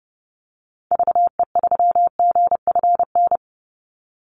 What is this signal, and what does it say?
Morse code, heard as a keyed tone, about 30 wpm, 700 Hz. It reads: VE3GFN
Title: Call Signs Repeated 2x - 30wpm